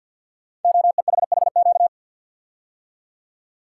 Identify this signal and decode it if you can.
Morse code, heard as a keyed tone, about 52 wpm, 700 Hz. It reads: OEHHQ
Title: Code Groups - 5 Letters - 50wpm